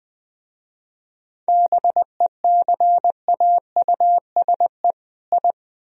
Morse code, heard as a keyed tone, about 24 wpm, 700 Hz. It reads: BECAUSE I
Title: Speed-Racing - Sets of 2 Words 20wpm